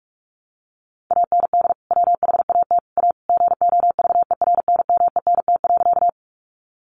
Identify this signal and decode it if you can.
Morse code, heard as a keyed tone, about 44 wpm, 700 Hz. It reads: AND WHAT A GOVERNMENT.
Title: Sentences from Top 600 Words 45wpm